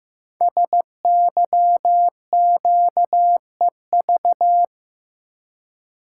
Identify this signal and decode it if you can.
Morse code, heard as a keyed tone, about 15 wpm, 700 Hz. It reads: SYQEV
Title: Code Groups - 5 Letters - 15wpm